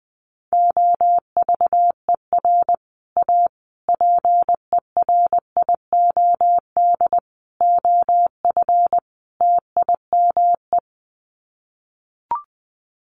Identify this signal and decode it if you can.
Morse code, heard as a keyed tone, about 20 wpm, 700 Hz. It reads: OVER A PERIOD OF TIME
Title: Sets of 5 Words Encore2 20wpm